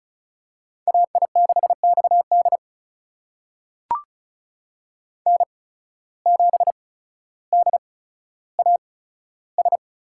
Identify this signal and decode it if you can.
Morse code, heard as a keyed tone, about 36 wpm, spaced as for 15 wpm, 700 Hz. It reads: AI6XD N7DAS
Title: USA Call Signs - 5 Characters - 8x Character Spacing - 35wpm